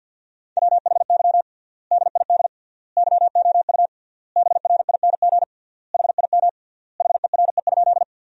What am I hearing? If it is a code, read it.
Morse code, heard as a keyed tone, about 49 wpm, 700 Hz. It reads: WHY DID YOU BRING HIM HERE?
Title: 3x Word-Spacing - Sentences from Top 500 Words - 50wpm